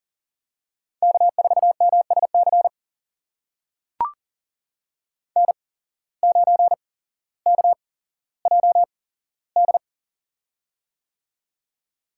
Morse code, heard as a keyed tone, about 40 wpm, spaced as for 17 wpm, 700 Hz. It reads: K4MSC N9KJD
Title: USA Call Signs - 5 Characters - 8x Character Spacing - 40wpm